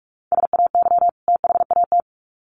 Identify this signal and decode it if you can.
Morse code, heard as a keyed tone, about 44 wpm, 700 Hz. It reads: SAY THAT
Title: Sets of 2 Words 45wpm